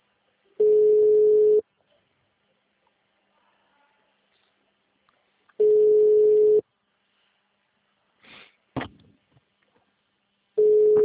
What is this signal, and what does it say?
Ringback tone (Europe, 425 Hz).
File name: out- -2004- - - .372